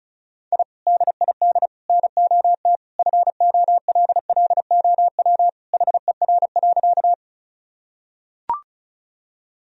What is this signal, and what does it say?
Morse code, heard as a keyed tone, about 36 wpm, 700 Hz. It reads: I DID NOT FOLLOW HER.
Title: Sentences from Top 1000 Words - Review Part 1 - 35wpm